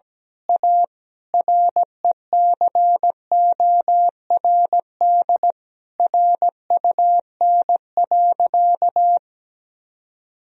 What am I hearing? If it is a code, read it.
Morse code, heard as a keyed tone, about 17 wpm, 700 Hz. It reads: A RECORD RUN.
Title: Sentences from Top 1000 Words - Review Part 3 - 17wpm